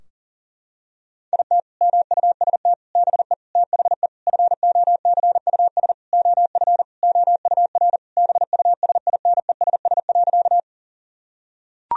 Morse code, heard as a keyed tone, about 40 wpm, 700 Hz. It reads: IT MUST BE THE FOCUS OF OUR BUSINESS.
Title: Sentences from Top 1000 Words - Review Part 4 - 40wpm